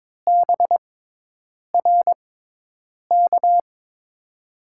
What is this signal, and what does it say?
Morse code, heard as a keyed tone, about 22 wpm, spaced as for 8 wpm, 700 Hz. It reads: BRK
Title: USA Call Signs - 6x Character Spacing - 22wpm